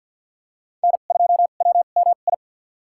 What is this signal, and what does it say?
Morse code, heard as a keyed tone, about 49 wpm, spaced as for 35 wpm, 700 Hz. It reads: N2WMI
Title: USA Call Signs - 5 Characters - 2x Character Spacing - 50wpm